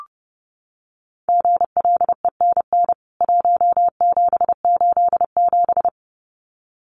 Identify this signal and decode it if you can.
Morse code, heard as a keyed tone, about 30 wpm, 700 Hz. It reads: GLENN 1787
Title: CWT Exchange 30wpm